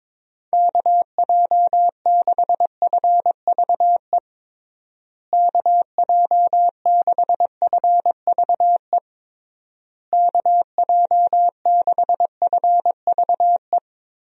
Morse code, heard as a keyed tone, about 22 wpm, 700 Hz. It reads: KJ6FVE KJ6FVE KJ6FVE
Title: USA Call Signs - 6 Characters - Repeated 3x - 22wpm